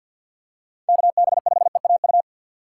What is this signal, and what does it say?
Morse code, heard as a keyed tone, about 51 wpm, 700 Hz. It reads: KB5EAU
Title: USA Call Signs - 6 Characters - 6x Character Spacing - 50wpm